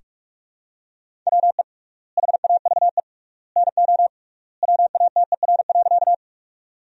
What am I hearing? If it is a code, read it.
Morse code, heard as a keyed tone, about 44 wpm, 700 Hz. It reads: WE HAVE NO WATER.
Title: 3x Word-Spacing - Sentences from Top 100 Words - 45wpm